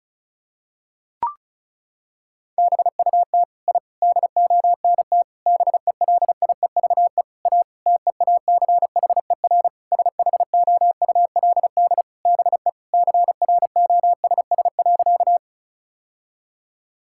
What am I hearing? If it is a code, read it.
Morse code, heard as a keyed tone, about 36 wpm, 700 Hz. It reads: BUT I DONT BELIEVE A TEACHER SHOULD BE CROSS.
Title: Sentences from Top 1000 Words - Review Part 6 - 35wpm